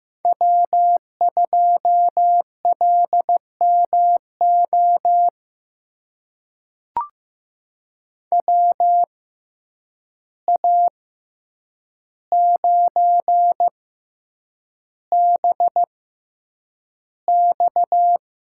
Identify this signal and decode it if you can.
Morse code, heard as a keyed tone, about 15 wpm, spaced as for 8 wpm, 700 Hz. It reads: W2LMO WA9BX
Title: USA Call Signs - 6x Character Spacing - 15wpm